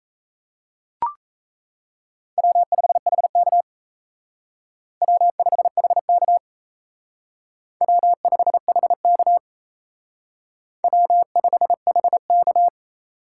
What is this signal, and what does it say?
Morse code, heard as a keyed tone, about 32 wpm, 700 Hz. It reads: W5HK W5HK W5HK W5HK
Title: Speed Racing - USA 1x2 Call Signs 28wpm